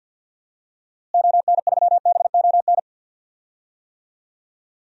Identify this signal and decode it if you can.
Morse code, heard as a keyed tone, about 52 wpm, 700 Hz. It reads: ON3BON
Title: DXCC-100 Call Signs - 6 Characters - Repeated 2x - 50wpm